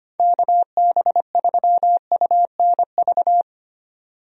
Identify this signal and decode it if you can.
Morse code, heard as a keyed tone, about 25 wpm, 700 Hz. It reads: KB3UNV
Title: USA Call Signs - 6 Characters - 25wpm